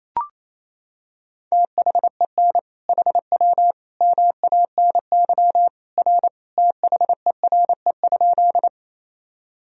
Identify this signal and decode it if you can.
Morse code, heard as a keyed tone, about 28 wpm, 700 Hz. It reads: THEN HW MANY R THERE?
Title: Sentences from Top 100 Words and Abbreviations 28wpm